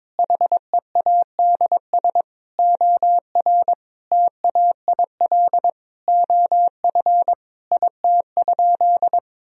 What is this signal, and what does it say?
Morse code, heard as a keyed tone, about 22 wpm, 700 Hz. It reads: HEADS OR TAIL OF IT?
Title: Sentences from Top 1000 Words - Review Part 3 - 22wpm